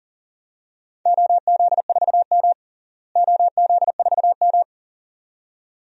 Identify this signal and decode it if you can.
Morse code, heard as a keyed tone, about 40 wpm, 700 Hz. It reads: OZ4M OZ4M
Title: DXCC-100 Call Signs - 4 Characters - Repeated 2x - 40wpm